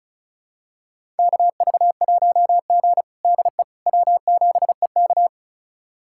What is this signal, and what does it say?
Morse code, heard as a keyed tone, about 36 wpm, 700 Hz. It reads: KV1G DE W7EK
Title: USA Callsign DE Callsign - 4 Characters - 35wpm